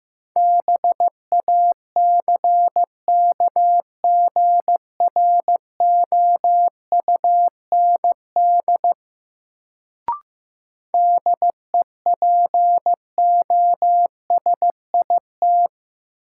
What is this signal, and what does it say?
Morse code, heard as a keyed tone, about 15 wpm, 700 Hz. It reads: BACKGROUND DEPOSIT
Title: Sentences from Top 1300 Words - New Word Drill - 15wpm